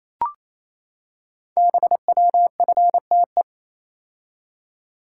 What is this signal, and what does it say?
Morse code, heard as a keyed tone, about 29 wpm, 700 Hz. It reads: BWFTE
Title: Code Groups - 5 Letters - 28wpm